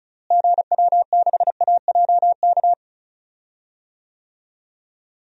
Morse code, heard as a keyed tone, about 34 wpm, 700 Hz. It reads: GW6AJK
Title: DXCC-100 Call Signs - 6 Characters - Repeated 2x - 35wpm